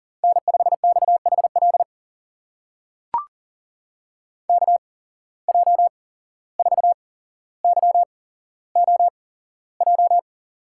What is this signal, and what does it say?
Morse code, heard as a keyed tone, about 40 wpm, spaced as for 16 wpm, 700 Hz. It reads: N5XHL KJ4YOJ
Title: USA Call Signs - 8x Character Spacing - 40wpm